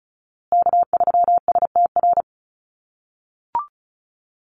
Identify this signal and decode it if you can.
Morse code, heard as a keyed tone, about 34 wpm, 700 Hz. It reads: K3STR
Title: USA Call Signs Repeated 2x 35wpm